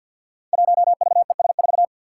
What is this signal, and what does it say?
Morse code, heard as a keyed tone, about 52 wpm, 700 Hz. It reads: 1VES4
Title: Five Challenging Letter-Number Combinations 50wpm